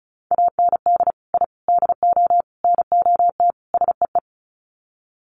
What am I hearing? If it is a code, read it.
Morse code, heard as a keyed tone, about 34 wpm, 700 Hz. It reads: AND I DO NOT SEE
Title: Sets of 5 Words Encore2 35wpm